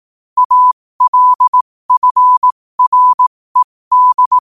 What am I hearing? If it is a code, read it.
Morse code, heard as a keyed tone, about 18 wpm, spaced as for 15 wpm, 1000 Hz. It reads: ALFRED
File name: qotdfiles / 15wpm